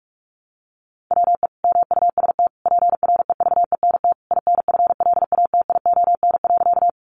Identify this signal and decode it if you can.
Morse code, heard as a keyed tone, about 46 wpm, 700 Hz. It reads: WE MUST PREVENT INFLATION.
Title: Sentences from Top 1100 Words 45wpm